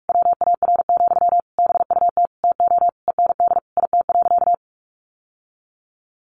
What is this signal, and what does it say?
Morse code, heard as a keyed tone, about 44 wpm, 700 Hz. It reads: WAR, BUT TO END IT.
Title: Sentences from Top 1200 Words 45wpm